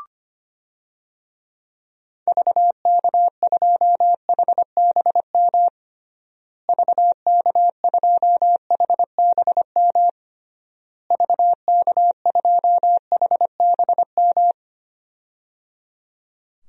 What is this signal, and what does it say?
Morse code, heard as a keyed tone, about 25 wpm, 700 Hz. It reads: VK2HBM VK2HBM VK2HBM
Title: DXCC-100 Call Signs - 6 Characters - Repeated 3x - 25wpm